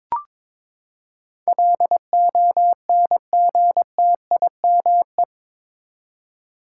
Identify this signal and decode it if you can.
Morse code, heard as a keyed tone, about 22 wpm, 700 Hz. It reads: LONGTIME
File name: practice / 22wpm / Eight Letter Words - 22wpm